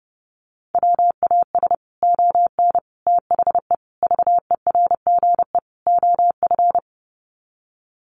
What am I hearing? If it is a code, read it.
Morse code, heard as a keyed tone, about 30 wpm, 700 Hz. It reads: WAS ON THE VERGE OF
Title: Sets of 5 Words Encore 30wpm